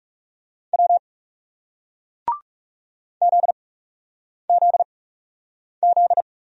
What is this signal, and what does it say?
Morse code, heard as a keyed tone, about 40 wpm, 700 Hz. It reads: W Z Z Z